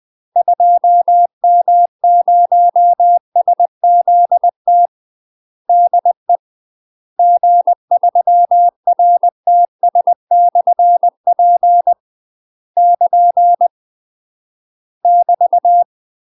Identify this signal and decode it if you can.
Morse code, heard as a keyed tone, about 20 wpm, 700 Hz. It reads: 2M0SZT DE G3RTS/P <KN> <BT>
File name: UK MASTER QSOs TRAINING/20WPM